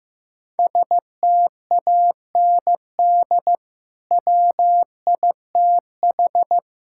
Morse code, heard as a keyed tone, about 15 wpm, 700 Hz. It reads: STAND WITH
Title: Sentences from Top 1600 Words 15wpm